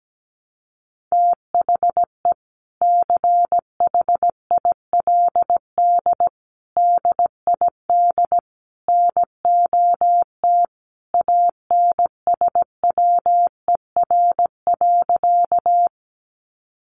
Morse code, heard as a keyed tone, about 17 wpm, 700 Hz. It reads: THE CHILD DID NOT ANSWER.
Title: Sentences from Top 800 Words 17wpm